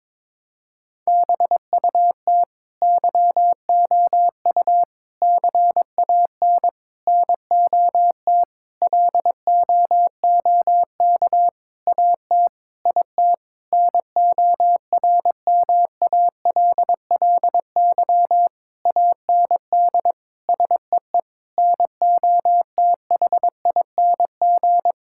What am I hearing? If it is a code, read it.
Morse code, heard as a keyed tone, about 22 wpm, 700 Hz. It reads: BUT YOU CAN NOT LOOK AT IT NORMALLY AND SEE NOTHING
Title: Sentences from Top 1000 Words - Review Part 6 - 22wpm